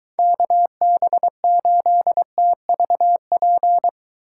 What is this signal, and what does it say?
Morse code, heard as a keyed tone, about 23 wpm, 700 Hz. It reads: KB8TVP